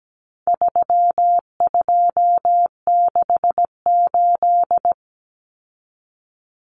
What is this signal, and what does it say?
Morse code, heard as a keyed tone, about 17 wpm, 700 Hz. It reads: 3268
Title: Numbers - 4-Digit 17wpm